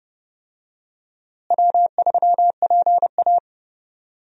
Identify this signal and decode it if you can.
Morse code, heard as a keyed tone, about 30 wpm, 700 Hz. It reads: W3PA
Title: USA Call Signs - 4 Characters - Repeated 3x - 30wpm